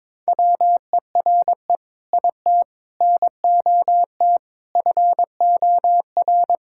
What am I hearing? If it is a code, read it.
Morse code, heard as a keyed tone, about 22 wpm, 700 Hz. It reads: WERE IT NOT FOR
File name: Sentences with 7-Word Phrases - 22wpm